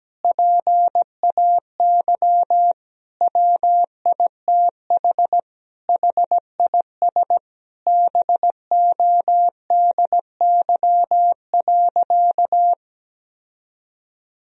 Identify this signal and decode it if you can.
Morse code, heard as a keyed tone, about 17 wpm, 700 Hz. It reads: PAY WITH HIS BODY.
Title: Sentences from Top 1000 Words - Review Part 2 - 17wpm